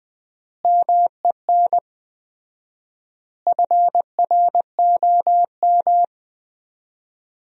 Morse code, heard as a keyed tone, about 20 wpm, 700 Hz. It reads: MEN FROM
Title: 4x Word-Spacing - Sentences from Top 300 Words - 20wpm